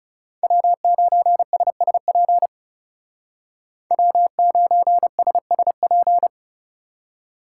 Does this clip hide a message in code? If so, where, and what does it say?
Yes: Morse code from 0.4 to 6.3 s, heard as a keyed tone, about 39 wpm, 700 Hz. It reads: W9SSP W9SSP